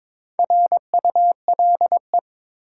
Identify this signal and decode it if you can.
Morse code, heard as a keyed tone, about 23 wpm, 700 Hz. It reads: RULE
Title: Sentences from Top 400 Words - New Word Drill - 22wpm